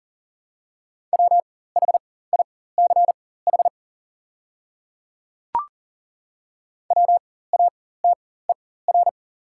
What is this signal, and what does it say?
Morse code, heard as a keyed tone, about 40 wpm, spaced as for 19 wpm, 700 Hz. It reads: WHICH WATER
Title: 4x Character Spacing - Top 200 Words - 40wpm